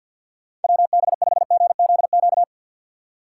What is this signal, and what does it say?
Morse code, heard as a keyed tone, about 53 wpm, 700 Hz. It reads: WB5GZQ
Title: USA Call Signs - 6 Characters - 6x Character Spacing - 50wpm